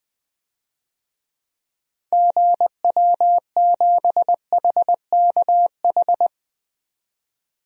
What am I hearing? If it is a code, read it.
Morse code, heard as a keyed tone, about 20 wpm, 700 Hz. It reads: GW7HKH
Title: DXCC-100 Call Signs - 6 Characters - 20wpm